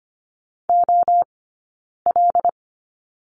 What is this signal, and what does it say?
Morse code, heard as a keyed tone, about 25 wpm, 700 Hz. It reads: O L
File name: course / 25:10wpm / 092 Lesson 16d - Letter c - Two Characters 25wpm